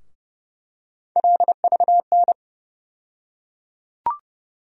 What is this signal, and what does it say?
Morse code, heard as a keyed tone, about 30 wpm, 700 Hz. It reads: LVN
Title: USA Call Sign Postfixes - 3 Letters 30wpm